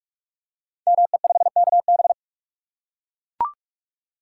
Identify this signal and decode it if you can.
Morse code, heard as a keyed tone, about 44 wpm, 690 Hz. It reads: ME5KB
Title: Code Groups - 5 Letter:Numbers - 45wpm